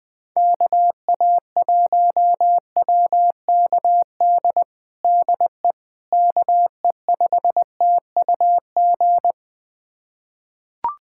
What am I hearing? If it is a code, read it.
Morse code, heard as a keyed tone, about 20 wpm, 700 Hz. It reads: KA1WKD DE KE5TUG
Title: USA Callsign DE Callsign - 6 Characters - 20wpm